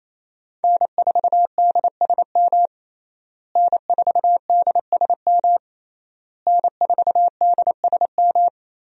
Morse code, heard as a keyed tone, about 27 wpm, 700 Hz. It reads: N4DSM N4DSM N4DSM
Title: USA Call Signs - 5 Characters - Repeated 3x - 28wpm